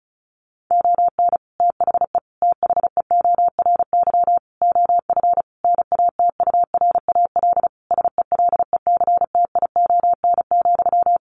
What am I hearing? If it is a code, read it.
Morse code, heard as a keyed tone, about 34 wpm, 700 Hz. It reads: ON THE THEORY OF NATURAL SELECTION,
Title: Sentences from Top 1400 Words 35wpm